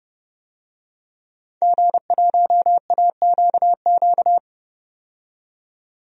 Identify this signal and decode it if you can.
Morse code, heard as a keyed tone, about 32 wpm, 700 Hz. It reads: G1AQQ
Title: DXCC-100 Call Signs - 5 Characters - 30wpm